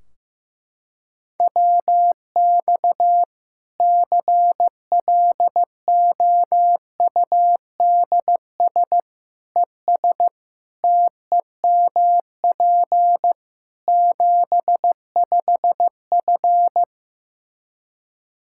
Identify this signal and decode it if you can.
Morse code, heard as a keyed tone, about 15 wpm, 700 Hz. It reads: WX CLOUDS ES TEMP 75F
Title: Practice QSO - Exchange 2 - Info - 15wpm